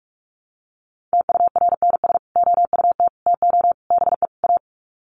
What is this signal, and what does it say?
Morse code, heard as a keyed tone, about 44 wpm, 700 Hz. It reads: TURNS OUT TO BE A
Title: Sets of 5 Words Encore 45wpm